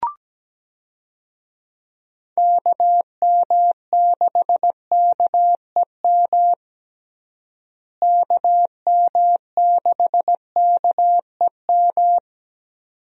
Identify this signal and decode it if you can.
Morse code, heard as a keyed tone, about 17 wpm, 700 Hz. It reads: KM6KEM KM6KEM
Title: USA Call Signs - 6 Characters - Repeated 2x - 17wpm